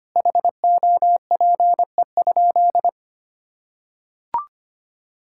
Morse code, heard as a keyed tone, about 25 wpm, 700 Hz. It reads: HOPE?
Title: Sentences from Top 1000 Words - Review Part 5 - 25wpm